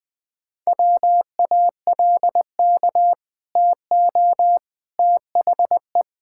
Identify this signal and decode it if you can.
Morse code, heard as a keyed tone, about 20 wpm, 700 Hz. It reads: WALK TO THE
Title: Easy Sentences with Hints - 20wpm